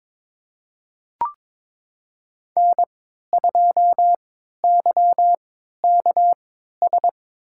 Morse code, heard as a keyed tone, about 22 wpm, spaced as for 12 wpm, 700 Hz. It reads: N2YKS
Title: USA Call Signs - 5 Characters - 3x Character Spacing - 22wpm